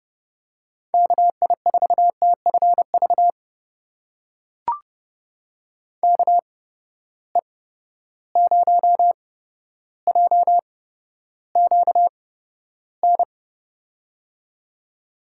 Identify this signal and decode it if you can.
Morse code, heard as a keyed tone, about 30 wpm, spaced as for 13 wpm, 700 Hz. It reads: KI4TFV KE0JQN
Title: USA Call Signs - 6 Characters - 8x Character Spacing - 30wpm